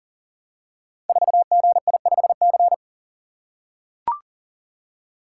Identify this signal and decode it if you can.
Morse code, heard as a keyed tone, about 42 wpm, 700 Hz. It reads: 4GI5C
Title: Code Groups - 5 Letter:Numbers - 40wpm